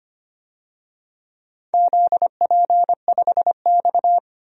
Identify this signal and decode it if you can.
Morse code, heard as a keyed tone, about 25 wpm, 700 Hz. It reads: ZP5X
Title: DXCC-100 Call Signs - 4 Characters - 25wpm